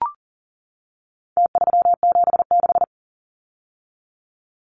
Morse code, heard as a keyed tone, about 40 wpm, 700 Hz. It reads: T376
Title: Numbers - Contest Style - 4-Digit 40wpm